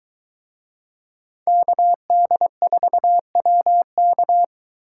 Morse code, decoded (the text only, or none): KD4WK